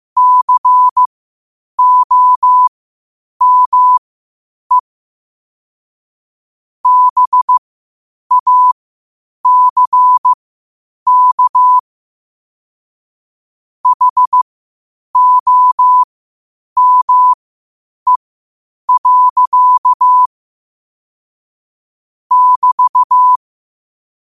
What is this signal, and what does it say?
Morse code, heard as a keyed tone, about 15 wpm, spaced as for 8 wpm, 1000 Hz. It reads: COME BACK HOME. <BT>
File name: qotdfiles / 8wpm